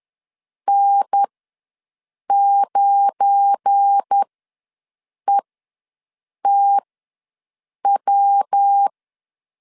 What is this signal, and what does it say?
Morse code, heard as a keyed tone, about 10 wpm, spaced as for 6 wpm, 800 Hz. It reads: N9ETW
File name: LEZIONI / 10 SIGNS / 10D COMA ws 10wpm 4314